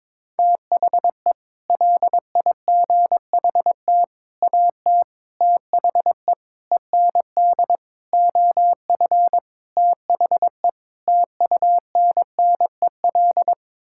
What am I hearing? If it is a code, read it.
Morse code, heard as a keyed tone, about 22 wpm, 700 Hz. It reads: THE LIGHT AT THE END OF THE TUNNEL
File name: practice / 22wpm / Sets of 8 Words - 22wpm